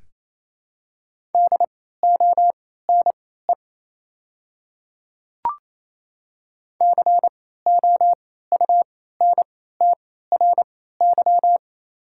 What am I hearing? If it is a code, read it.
Morse code, heard as a keyed tone, about 27 wpm, spaced as for 16 wpm, 700 Hz. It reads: DONE COUNTRY
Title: 3x Character Spacing - Top 500 Words - 28wpm